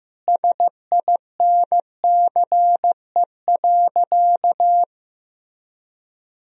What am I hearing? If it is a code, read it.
Morse code, heard as a keyed tone, about 16 wpm, 700 Hz. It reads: SINCE.
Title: Sentences from Top 400 Words and Abbreviations 15wpm